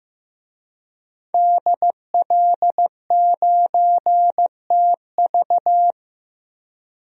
Morse code, heard as a keyed tone, about 15 wpm, 700 Hz. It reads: DL9TV
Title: DXCC-100 Call Signs - 5 Characters - 15wpm